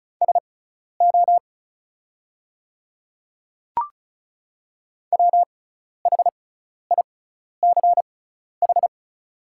Morse code, heard as a keyed tone, about 34 wpm, spaced as for 12 wpm, 690 Hz. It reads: SO WHICH
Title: 6x Character Spacing - Top 100 Words - 35wpm